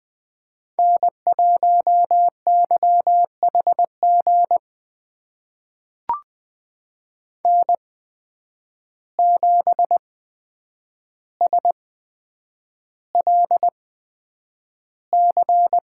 Morse code, heard as a keyed tone, about 20 wpm, spaced as for 9 wpm, 700 Hz. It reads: N1YHG N7SLC
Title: USA Call Signs - 5 Characters - 8x Character Spacing - 20wpm